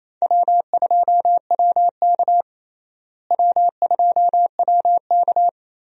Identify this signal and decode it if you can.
Morse code, heard as a keyed tone, about 28 wpm, 700 Hz. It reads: W2WK W2WK
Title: USA Call Signs - 4 Characters - Repeated 3x - 28wpm